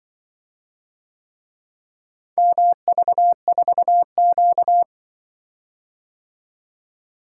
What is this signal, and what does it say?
Morse code, heard as a keyed tone, about 25 wpm, 700 Hz. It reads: MV4Q
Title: Speed-Racing - Four Letter-Number Combinations - 20wpm